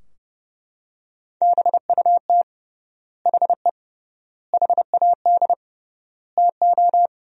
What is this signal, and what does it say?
Morse code, heard as a keyed tone, about 30 wpm, 700 Hz. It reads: BUT HE HAD TO
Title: 3x Word-Spacing - Sentences from Top 100 Words - 30wpm